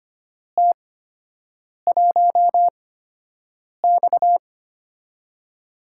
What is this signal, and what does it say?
Morse code, heard as a keyed tone, about 25 wpm, spaced as for 7 wpm, 700 Hz. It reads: T1X
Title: USA Call Signs - 4 Characters - 8x Character Spacing - 25wpm